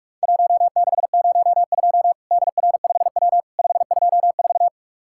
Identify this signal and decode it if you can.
Morse code, heard as a keyed tone, about 44 wpm, 700 Hz. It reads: 1602 DR5W 524
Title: Worked All Europe DX Contest - QTC Exchange - 45wpm